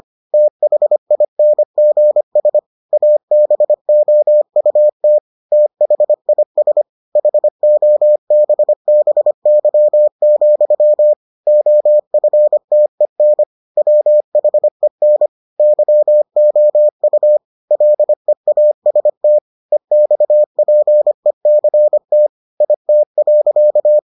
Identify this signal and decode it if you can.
Morse code, heard as a keyed tone, about 25 wpm, 600 Hz. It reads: THINGS ABOUT THIS HOBBY, OFTEN WHEN YOU LEAST EXPECT IT.